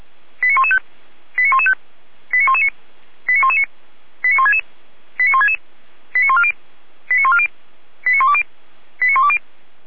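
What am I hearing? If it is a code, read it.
Five-tone selective call (ZVEI), heard as quick runs of five tones, tones of about 80 ms. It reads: 89115 89115 89119 89119 89166 89166 89140 89140 89120 89120
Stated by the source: ZVEI five-tone call 89110 received on SDR.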